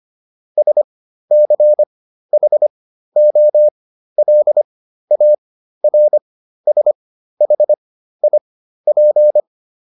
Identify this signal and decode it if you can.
Morse code, heard as a keyed tone, about 25 wpm, spaced as for 13 wpm, 600 Hz. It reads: SCHOLARSHIP